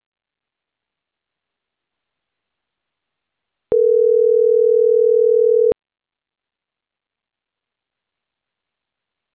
Ringback tone (North America), 440 + 480 Hz.